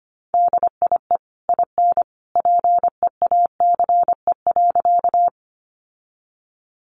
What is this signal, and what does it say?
Morse code, heard as a keyed tone, about 25 wpm, 700 Hz. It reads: DIE IN PEACE.
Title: Sentences from Top 1500 Words 25wpm